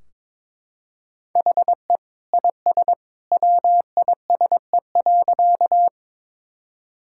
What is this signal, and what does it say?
Morse code, heard as a keyed tone, about 22 wpm, 700 Hz. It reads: HE IS WISE.
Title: Sentences from Top 1600 Words 22wpm